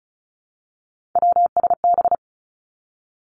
Morse code, heard as a keyed tone, about 36 wpm, 700 Hz. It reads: WSB
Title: Three Challenging Letter-Number Combinations 35wpm